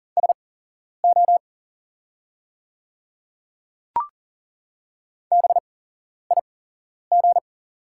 Morse code, heard as a keyed tone, about 40 wpm, spaced as for 11 wpm, 690 Hz. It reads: SO BIG
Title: 8x Character Spacing - Top 200 Words - 40wpm